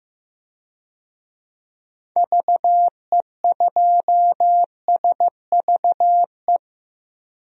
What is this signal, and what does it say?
Morse code, heard as a keyed tone, about 15 wpm, 700 Hz. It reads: VE2SVE
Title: DXCC-100 Call Signs - 6 Characters - 15wpm